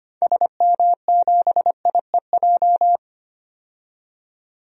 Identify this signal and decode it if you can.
Morse code, heard as a keyed tone, about 25 wpm, 690 Hz. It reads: SM7IEJ
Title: DXCC-100 Call Signs - 6 Characters - 25wpm